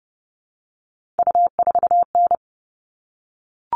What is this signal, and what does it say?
Morse code, heard as a keyed tone, about 30 wpm, 700 Hz. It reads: U4N
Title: All Cut Numbers - 3-Digit 30wpm